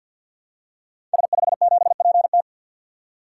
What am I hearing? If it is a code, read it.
Morse code, heard as a keyed tone, about 52 wpm, 700 Hz. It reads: S57PT
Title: Code Groups - 5 Letter:Numbers - 50wpm